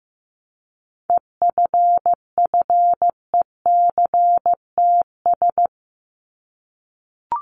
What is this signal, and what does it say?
Morse code, heard as a keyed tone, about 15 wpm, 700 Hz. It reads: EFFECTS
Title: ES:S Suffix - Top 100 Words with Base Words 15wpm